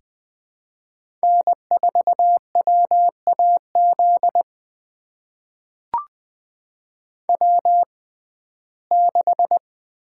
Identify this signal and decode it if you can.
Morse code, heard as a keyed tone, about 20 wpm, spaced as for 15 wpm, 700 Hz. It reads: N4WAZ W6
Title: USA Call Signs - 6x Character Spacing - 20wpm